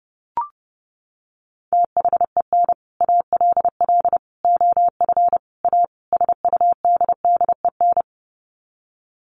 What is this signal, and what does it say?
Morse code, heard as a keyed tone, about 30 wpm, 700 Hz. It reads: THEN ALL OF A SUDDEN
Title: Sets of 5 Words Encore 30wpm